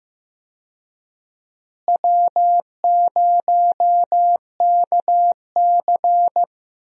Morse code, heard as a keyed tone, about 15 wpm, 700 Hz. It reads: W0KC
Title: USA Call Signs - 4 Characters - Repeated 3x - 15wpm